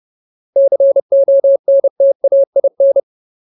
Morse code, heard as a keyed tone, about 30 wpm, 550 Hz. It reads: CONTAIN
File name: Ultimate Word List / 30WPM Ultimate